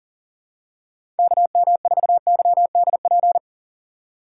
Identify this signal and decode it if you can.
Morse code, heard as a keyed tone, about 40 wpm, 700 Hz. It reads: KM4YDP
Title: USA Call Signs - 6 Characters - 4x Character Spacing - 40wpm